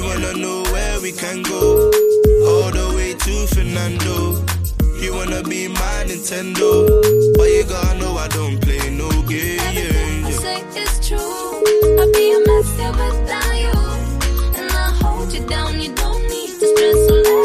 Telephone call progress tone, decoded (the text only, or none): ringback tone